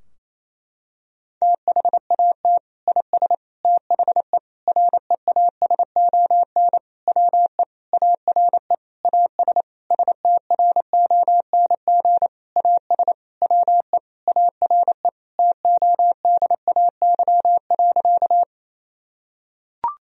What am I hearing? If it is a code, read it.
Morse code, heard as a keyed tone, about 28 wpm, 700 Hz. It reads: THAT IS THE REASON WE ARE AS STRONG AS WE ARE TODAY.
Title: Sentences from Top 1000 Words - Review Part 4 - 28wpm